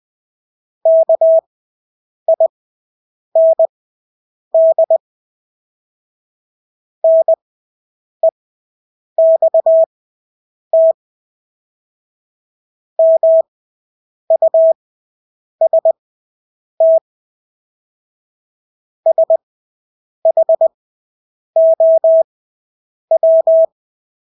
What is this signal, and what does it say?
Morse code, heard as a keyed tone, about 20 wpm, spaced as for 8 wpm, 650 Hz. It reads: KIND NEXT MUST SHOW
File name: Engelske ord 20/8 wpm